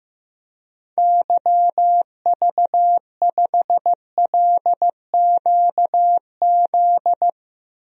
Morse code, heard as a keyed tone, about 15 wpm, 700 Hz. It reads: YV5LQZ
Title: DXCC-100 Call Signs - 6 Characters - Repeated 2x - 15wpm